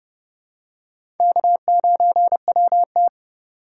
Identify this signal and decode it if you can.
Morse code, heard as a keyed tone, about 30 wpm, 700 Hz. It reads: K9WT
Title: USA Call Signs - 4 Characters - 4x Character Spacing - 30wpm